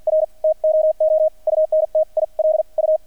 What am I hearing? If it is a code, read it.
Morse code, heard as a keyed tone, about 50 wpm, spaced as for 42 wpm, 630 Hz. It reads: R TOO UNTIL U
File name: archived / CRSnet